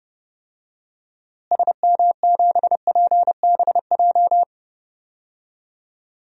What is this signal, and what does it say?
Morse code, heard as a keyed tone, about 30 wpm, 700 Hz. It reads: SM7PBJ
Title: DXCC-100 Call Signs - 6 Characters - Repeated 2x - 30wpm